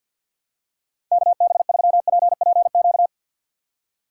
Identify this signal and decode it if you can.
Morse code, heard as a keyed tone, about 53 wpm, 700 Hz. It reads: KB3PPQ
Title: USA Call Signs - 6 Characters - 3x Character Spacing - 50wpm